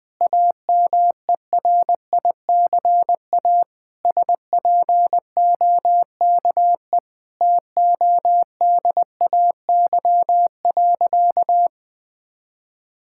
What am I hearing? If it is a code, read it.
Morse code, heard as a keyed tone, about 20 wpm, 700 Hz. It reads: AMERICA SPOKE TODAY.
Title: Sentences with 8-Word Phrases - 20wpm